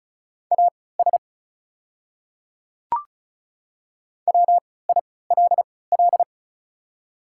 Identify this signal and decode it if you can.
Morse code, heard as a keyed tone, about 36 wpm, spaced as for 20 wpm, 690 Hz. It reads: AS WILL
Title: 3x Character Spacing - Top 100 Words - 35wpm